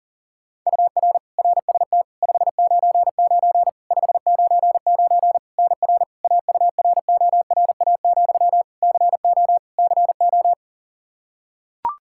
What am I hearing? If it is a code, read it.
Morse code, heard as a keyed tone, about 37 wpm, 700 Hz. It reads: UR RST 599 599 NR AURORA, CO CO
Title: K1USN SST QSO Practice 40wpm